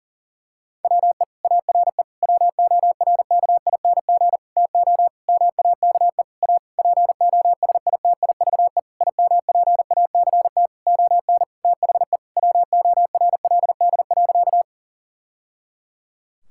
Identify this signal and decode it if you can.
Morse code, heard as a keyed tone, about 40 wpm, 700 Hz. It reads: WE ARE WORKING TO MAKE A POSITIVE IMPACT ON THE WORLD.
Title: Sentences from Top 1000 Words - Review Part 6 - 40wpm